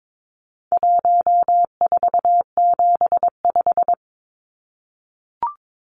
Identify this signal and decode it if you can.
Morse code, heard as a keyed tone, about 22 wpm, 700 Hz. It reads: 1475
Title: Numbers - 4-Digit 22wpm